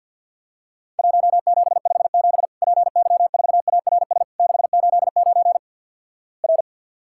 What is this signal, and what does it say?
Morse code, heard as a keyed tone, about 50 wpm, 700 Hz. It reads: 1757 PY4ARS 689 R
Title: Worked All Europe DX Contest - QTC Exchange - 50wpm